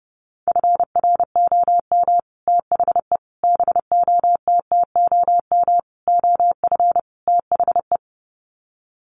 Morse code, heard as a keyed tone, about 30 wpm, 700 Hz. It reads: FROM THE BOTTOM OF THE
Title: Sets of 5 Words Encore2 30wpm